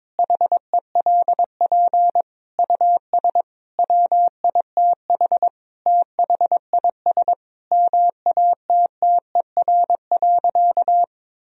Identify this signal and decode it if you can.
Morse code, heard as a keyed tone, about 22 wpm, 700 Hz. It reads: HELP US WITH THIS MATTER.
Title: Sentences from Top 1000 Words - Review Part 2 - 22wpm